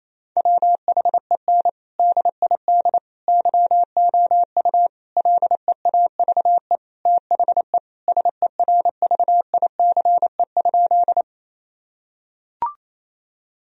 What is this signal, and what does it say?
Morse code, heard as a keyed tone, about 28 wpm, 700 Hz. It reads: WHEN DID YOU LEAVE THE SERVICE?
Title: Sentences from Top 1000 Words - Review Part 4 - 28wpm